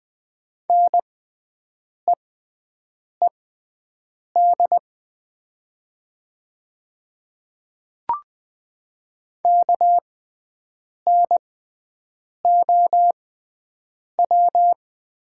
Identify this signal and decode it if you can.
Morse code, heard as a keyed tone, about 20 wpm, spaced as for 7 wpm, 700 Hz. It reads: NEED KNOW
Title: 6x Character Spacing - Top 300 Words - 20wpm